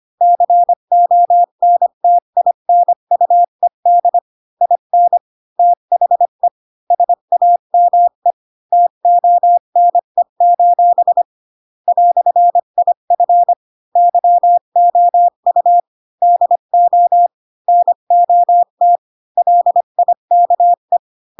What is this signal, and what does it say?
Morse code, heard as a keyed tone, about 25 wpm, 700 Hz. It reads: CONTINUED IN THE SAME TONE: "IF YOU DO NOT LIKE
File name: Pinocchio / Pinocchio ch4 20 25 ik1hge